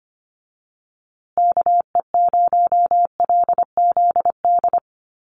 Morse code, heard as a keyed tone, about 25 wpm, 700 Hz. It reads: KE0LZD
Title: USA Call Signs Repeated 2x 25wpm